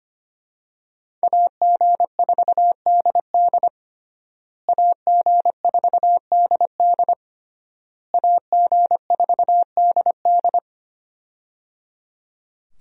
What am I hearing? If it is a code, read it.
Morse code, heard as a keyed tone, about 25 wpm, 700 Hz. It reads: AG4DD AG4DD AG4DD
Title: USA Call Signs - 5 Characters - Repeated 3x - 25wpm